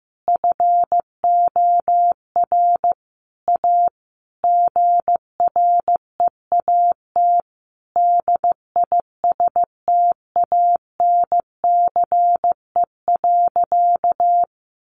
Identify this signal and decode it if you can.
Morse code, heard as a keyed tone, about 15 wpm, 700 Hz. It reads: FOR A GREAT DISTANCE.
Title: Sentences from Top 1200 Words 15wpm